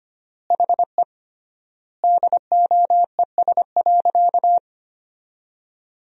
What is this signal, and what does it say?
Morse code, heard as a keyed tone, about 25 wpm, 700 Hz. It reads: HE DOES.
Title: 3x Word-Spacing - Sentences from Top 500 Words - 25wpm